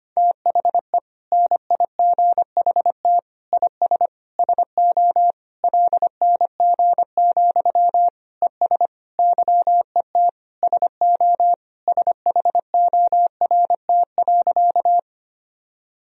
Morse code, heard as a keyed tone, about 25 wpm, 700 Hz. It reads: THE NIGHT IS SO LNG, ES YET SO SHORT.
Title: Sentences from Top 500 Words and Abbreviations 25wpm